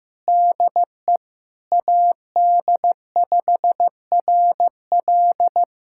Morse code, heard as a keyed tone, about 15 wpm, 700 Hz. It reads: DE AD5RL
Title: USA Callsign DE Callsign - 5 Characters - 15wpm